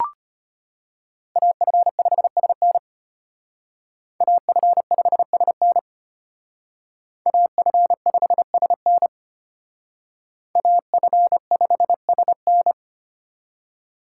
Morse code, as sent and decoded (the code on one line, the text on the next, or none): .- ..-. ..... ... -. / .- ..-. ..... ... -. / .- ..-. ..... ... -. / .- ..-. ..... ... -.
AF5SN AF5SN AF5SN AF5SN